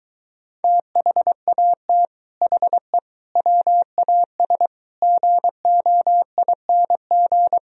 Morse code, heard as a keyed tone, about 22 wpm, 700 Hz. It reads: THAT HE WAS GOING